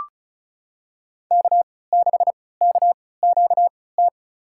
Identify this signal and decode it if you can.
Morse code, heard as a keyed tone, about 34 wpm, spaced as for 20 wpm, 700 Hz. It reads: K6KQT
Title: USA Call Signs - 5 Characters - 3x Character Spacing - 35wpm